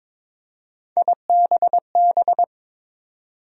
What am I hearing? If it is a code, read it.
Morse code, heard as a keyed tone, about 22 wpm, 700 Hz. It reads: IBB
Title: USA Call Sign Postfixes - 3 Letters 22wpm